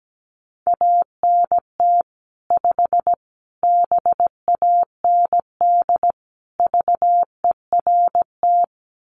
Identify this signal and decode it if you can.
Morse code, heard as a keyed tone, about 17 wpm, 700 Hz. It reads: ANT 5 BAND VERT
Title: Common QSO Phrases 17wpm